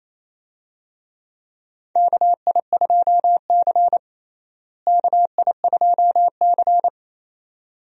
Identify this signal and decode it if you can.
Morse code, heard as a keyed tone, about 28 wpm, 700 Hz. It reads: KI2C KI2C
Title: USA Call Signs - 4 Characters - Repeated 2x - 28wpm